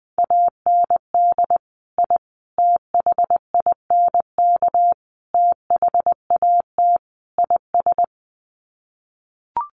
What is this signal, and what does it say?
Morse code, heard as a keyed tone, about 20 wpm, 700 Hz. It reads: AND I THINK THAT IS
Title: Sets of 5 Words Encore2 20wpm